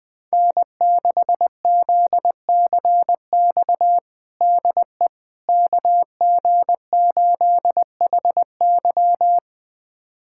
Morse code, heard as a keyed tone, about 20 wpm, 700 Hz. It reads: N6ZCX DE KG8HY
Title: USA Callsign DE Callsign 20wpm